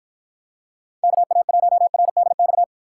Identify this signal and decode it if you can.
Morse code, heard as a keyed tone, about 53 wpm, 700 Hz. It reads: KA1RDX